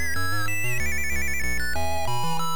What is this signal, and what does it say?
There is no code.